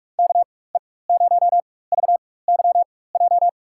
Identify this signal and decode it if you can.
Morse code, heard as a keyed tone, about 47 wpm, spaced as for 21 wpm, 700 Hz. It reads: KE0VYJ
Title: USA Call Signs - 6 Characters - 4x Character Spacing - 45wpm